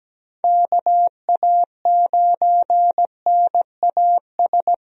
Morse code, heard as a keyed tone, about 17 wpm, 700 Hz. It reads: KA9NAS
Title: USA Call Signs - 6 Characters - Repeated 2x - 17wpm